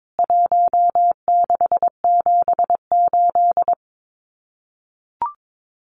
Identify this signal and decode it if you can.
Morse code, heard as a keyed tone, about 22 wpm, 700 Hz. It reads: 1678
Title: CWT Exchange 22wpm